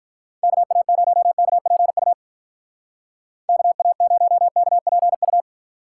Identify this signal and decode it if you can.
Morse code, heard as a keyed tone, about 46 wpm, 700 Hz. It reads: KA0KPU KA0KPU